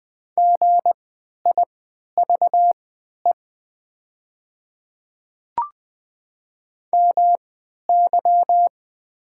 Morse code, heard as a keyed tone, about 20 wpm, spaced as for 11 wpm, 700 Hz. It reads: GIVE MY
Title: 3x Character Spacing - Top 200 Words - 20wpm